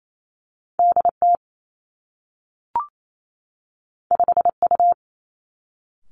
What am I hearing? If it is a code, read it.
Morse code, heard as a keyed tone, about 28 wpm, 700 Hz. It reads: D T 5 U
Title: Two Challenging Letter-Number Combinations 28wpm